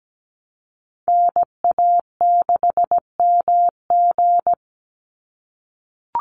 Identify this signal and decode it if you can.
Morse code, heard as a keyed tone, about 17 wpm, 700 Hz. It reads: NA6MG
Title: NA SOTA Chaser Callsigns 17wpm